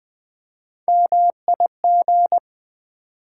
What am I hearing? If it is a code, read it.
Morse code, heard as a keyed tone, about 20 wpm, 690 Hz. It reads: MIG
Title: Build A Word - 1 Letter At A Time - 5-Letter Words - 20wpm